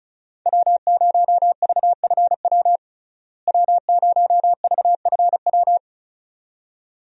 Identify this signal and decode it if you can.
Morse code, heard as a keyed tone, about 36 wpm, 700 Hz. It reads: W0VFW W0VFW
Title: USA Call Signs - 5 Characters - Repeated 2x - 35wpm